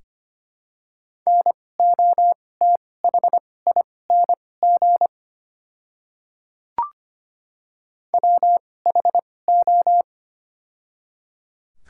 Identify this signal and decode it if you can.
Morse code, heard as a keyed tone, about 25 wpm, spaced as for 18 wpm, 700 Hz. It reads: NOTHING WHO
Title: 2x Character Spacing - Top 500 Words - 25wpm